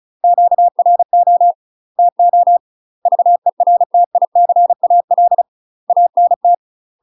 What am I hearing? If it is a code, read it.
Morse code, heard as a keyed tone, about 36 wpm, 700 Hz. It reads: QRO TO VERTICAL ANT
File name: UK MASTER QSOs TRAINING/35WPM